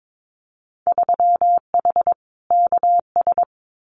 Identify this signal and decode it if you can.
Morse code, heard as a keyed tone, about 22 wpm, 700 Hz. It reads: 3H KS
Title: Winter Field Day Exchanges 22wpm